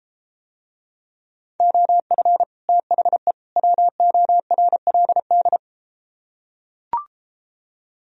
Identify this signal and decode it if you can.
Morse code, heard as a keyed tone, about 33 wpm, 700 Hz. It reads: OF THE WORLD